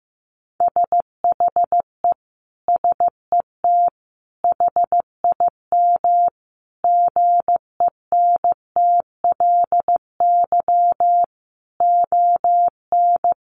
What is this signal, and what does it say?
Morse code, heard as a keyed tone, about 15 wpm, 700 Hz. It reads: SHE SET HIM GENTLY ON
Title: Sentences from Top 1500 Words 15wpm